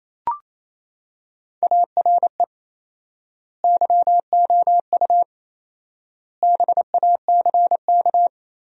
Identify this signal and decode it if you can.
Morse code, heard as a keyed tone, about 28 wpm, 700 Hz. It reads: ARE YOU BACK
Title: 4x Word-Spacing - Sentences from Top 200 Words - 28wpm